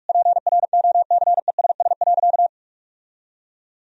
Morse code, heard as a keyed tone, about 46 wpm, 700 Hz. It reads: PROCESS.
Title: Sentences from Top 1000 Words - Review Part 3 - 45wpm